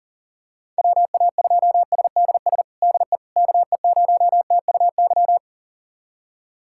Morse code, heard as a keyed tone, about 40 wpm, 700 Hz. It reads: WA2SDS DE KE0TUY
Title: USA Callsign DE Callsign - 6 Characters - 40wpm